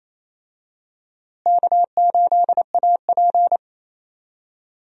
Morse code, heard as a keyed tone, about 27 wpm, 700 Hz. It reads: K8AP
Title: USA Call Signs - 4 Characters - Repeated 2x - 28wpm